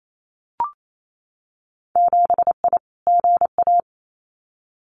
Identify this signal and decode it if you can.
Morse code, heard as a keyed tone, about 28 wpm, 700 Hz. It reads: 7I GA
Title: Winter Field Day Exchanges Encore 28wpm